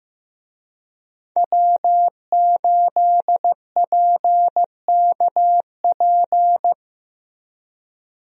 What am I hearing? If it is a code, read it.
Morse code, heard as a keyed tone, about 15 wpm, 700 Hz. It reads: W8PKP
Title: USA Call Signs - 5 Characters - Repeated 2x - 15wpm